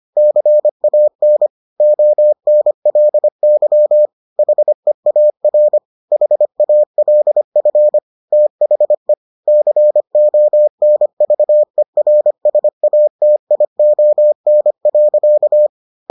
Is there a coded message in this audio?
Yes: Morse code from 0.2 to 15.7 s, heard as a keyed tone, about 25 wpm, 600 Hz. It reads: CAN ONLY HEAR HALF THE CONVERSATION.